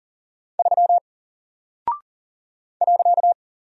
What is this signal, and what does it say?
Morse code, heard as a keyed tone, about 38 wpm, 700 Hz. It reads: 3 .
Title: Single Character (Letter:Number:Punctuation:Prosign) 40wpm